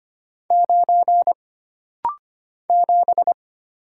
Morse code, heard as a keyed tone, about 25 wpm, 700 Hz. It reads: 9 7
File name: Single Character (Letter:Number:Punctuation:Prosign) - Rapid-Fire - 25wpm